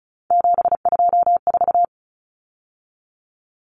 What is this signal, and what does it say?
Morse code, heard as a keyed tone, about 34 wpm, 700 Hz. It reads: 724
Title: Numbers - 3-Digit 35wpm